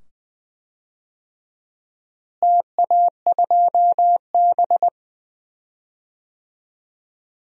Morse code, heard as a keyed tone, about 20 wpm, 700 Hz. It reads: TA2B
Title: DXCC-100 Call Signs - 4 Characters - Repeated 2x - 20wpm